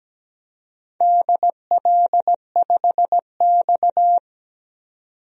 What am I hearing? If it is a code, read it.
Morse code, heard as a keyed tone, about 17 wpm, 700 Hz. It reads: DL5X
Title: DXCC-100 Call Signs - 4 Characters - 17wpm